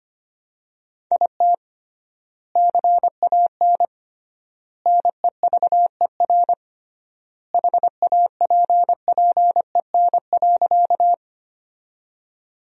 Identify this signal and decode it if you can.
Morse code, heard as a keyed tone, about 25 wpm, 700 Hz. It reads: IT CAN NEVER HAPPEN.
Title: 3x Word-Spacing - Sentences from Top 400 Words - 25wpm